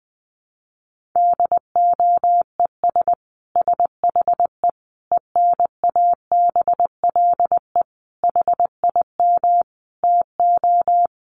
Morse code, heard as a keyed tone, about 20 wpm, 700 Hz. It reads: DOES SHE ENABLE HIM TO
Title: Sentences from Top 1500 Words 20wpm